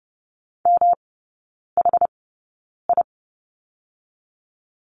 Morse code, heard as a keyed tone, about 30 wpm, 700 Hz. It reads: M H I
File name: Lesson 18e - Letter m - Three Characters 30wpm